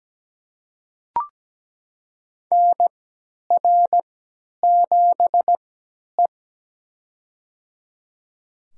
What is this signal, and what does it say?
Morse code, heard as a keyed tone, about 17 wpm, spaced as for 10 wpm, 700 Hz. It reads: NR7E
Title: USA Call Signs - 4 Characters - 3x Character Spacing - 17wpm